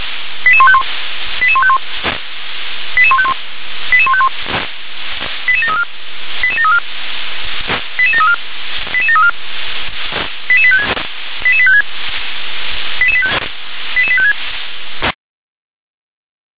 Five-tone selective call (ZVEI), heard as quick runs of five tones, tones of about 70 ms. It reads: 88151 88151 88151 88151 88734 88734 88734 88734 88856 88856 88856